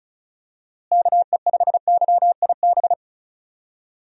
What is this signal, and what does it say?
Morse code, heard as a keyed tone, about 36 wpm, 700 Hz. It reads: KE5YIB
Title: USA Call Signs - 6 Characters - Repeated 3x - 35wpm